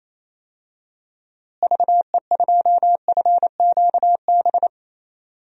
Morse code, heard as a keyed tone, about 28 wpm, 700 Hz. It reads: VE2FQB
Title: DXCC-100 Call Signs - 6 Characters - Repeated 2x - 28wpm